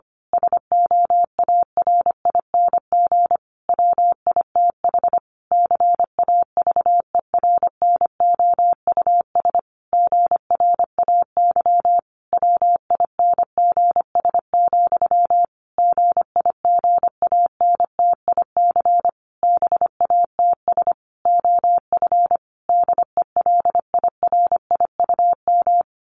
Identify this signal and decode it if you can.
Morse code, heard as a keyed tone, about 25 wpm, 700 Hz. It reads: SOARING WITH CAVERNOUS GRAY WINGS, GIGANTIC BATS OF DELIRIUM
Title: A Sound Of Thunder 25wpm